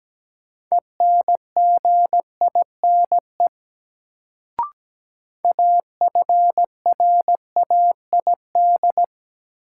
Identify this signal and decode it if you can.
Morse code, heard as a keyed tone, about 17 wpm, 700 Hz. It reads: ENGINE AFRAID
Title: Sentences from Top 1500 Words - New Word Drill - 17wpm